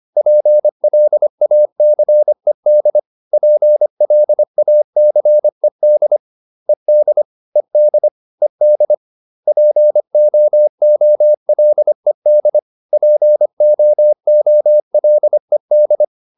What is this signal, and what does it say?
Morse code, heard as a keyed tone, about 25 wpm, 600 Hz. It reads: PLACED PLACED ED ED ED POOLED POOLED